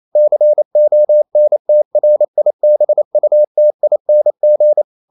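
Morse code, heard as a keyed tone, about 28 wpm, 600 Hz. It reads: CONTRIBUTING
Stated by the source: Morse code practice at 28 wpm, text of Ken Follett's novel Never in English, from CW med Gnister.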